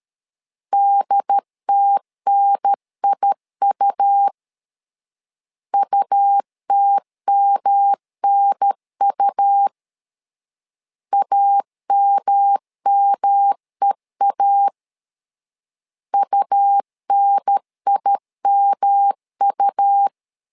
Morse code, heard as a keyed tone, about 13 wpm, 800 Hz. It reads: DTNIU UTMNU AMMEA UNIMU
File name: LEZIONI / 3 MWRO / 3 M 12wpm 1541